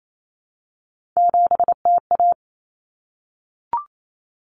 Morse code, heard as a keyed tone, about 28 wpm, 700 Hz. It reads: 7TA
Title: Common Cut Numbers - 3-Digit 28wpm